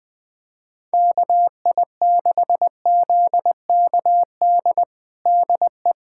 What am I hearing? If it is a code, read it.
Morse code, heard as a keyed tone, about 20 wpm, 700 Hz. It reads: KI6ZKD DE
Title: USA Callsign DE Callsign 20wpm